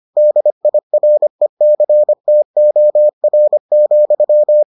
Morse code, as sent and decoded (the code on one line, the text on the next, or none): -.. .. .-. -.-. - --- .-. --..--
DIRECTOR,